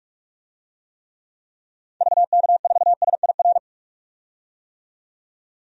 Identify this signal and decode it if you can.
Morse code, heard as a keyed tone, about 46 wpm, 700 Hz. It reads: VK4SIR
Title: DXCC-100 Call Signs - 6 Characters - 45wpm